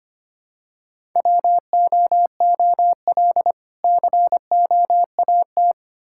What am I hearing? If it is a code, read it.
Morse code, heard as a keyed tone, about 25 wpm, 700 Hz. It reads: WOOL COAT
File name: practice / 25wpm / Difficult Words with Hints - 25wpm